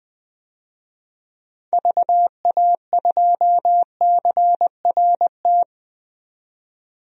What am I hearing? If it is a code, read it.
Morse code, heard as a keyed tone, about 20 wpm, 700 Hz. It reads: VA2CRT
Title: DXCC-100 Call Signs - 6 Characters - 20wpm